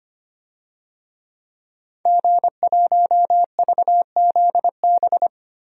Morse code, heard as a keyed tone, about 25 wpm, 700 Hz. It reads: G1VZB
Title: DXCC-100 Call Signs - 5 Characters - Repeated 3x - 25wpm